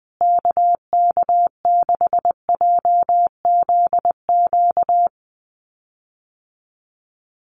Morse code, heard as a keyed tone, about 19 wpm, 700 Hz. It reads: KK6JZQ
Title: USA Call Signs Repeated 2x 20wpm